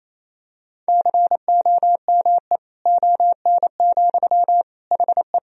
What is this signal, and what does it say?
Morse code, heard as a keyed tone, about 28 wpm, 700 Hz. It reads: COME ON, HE
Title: Sentences from Top 1000 Words - Review Part 2 - 28wpm